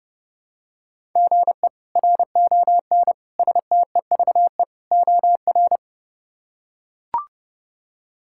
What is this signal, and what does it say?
Morse code, heard as a keyed tone, about 30 wpm, 700 Hz. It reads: GE RON STEVE OR
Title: K1USN SST QSO Practice - 30wpm